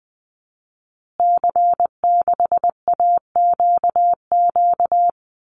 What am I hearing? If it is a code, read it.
Morse code, heard as a keyed tone, about 20 wpm, 700 Hz. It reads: C6AQQ
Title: Call Signs Repeated 2x - 20wpm